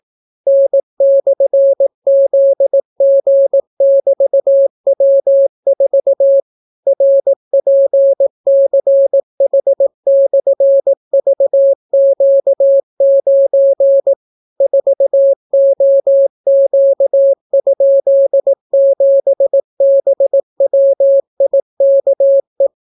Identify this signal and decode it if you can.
Morse code, heard as a keyed tone, about 18 wpm, 550 Hz. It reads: N/ZG<BT>W4 RPCH/VQ9 4OQ?7BWIKE